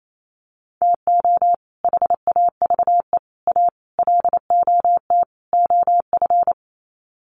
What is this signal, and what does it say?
Morse code, heard as a keyed tone, about 28 wpm, 700 Hz. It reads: TO HAVE A LOT OF
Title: Sets of 5 Words Encore 28wpm